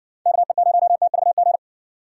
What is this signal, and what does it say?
Morse code, heard as a keyed tone, about 57 wpm, 700 Hz. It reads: KE0TVG